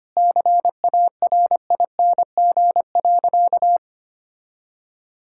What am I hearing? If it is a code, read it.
Morse code, heard as a keyed tone, about 25 wpm, 700 Hz. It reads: CARING.
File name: practice / 25wpm / Sentences with 2-Word Phrases - 25wpm